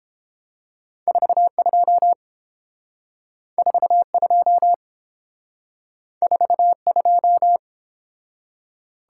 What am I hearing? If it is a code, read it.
Morse code, heard as a keyed tone, about 30 wpm, 700 Hz. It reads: 42 42 42